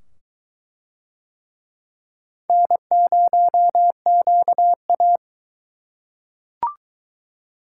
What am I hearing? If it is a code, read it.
Morse code, heard as a keyed tone, about 22 wpm, 700 Hz. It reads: N0QA